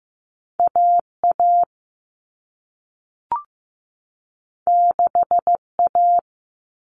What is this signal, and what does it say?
Morse code, heard as a keyed tone, about 15 wpm, 700 Hz. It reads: AA 6A
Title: All Cut Numbers - 2-Digit 15wpm